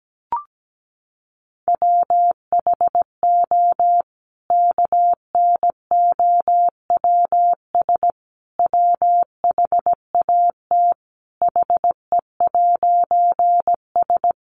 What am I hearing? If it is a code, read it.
Morse code, heard as a keyed tone, about 17 wpm, 700 Hz. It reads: WHO KNOWS WHAT HE'S
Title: A Sound Of Thunder 17wpm